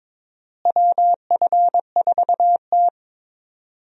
Morse code, heard as a keyed tone, about 22 wpm, 700 Hz. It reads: WF4T
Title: USA Call Signs - 4 Characters - Repeated 3x - 22wpm